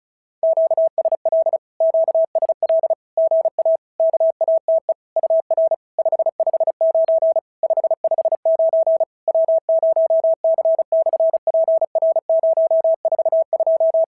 Morse code, heard as a keyed tone, about 36 wpm, 690 Hz. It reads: QSL QSL GA KATE UR 559 559 W0C/PR042
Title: SOTA QSO Practice 35wpm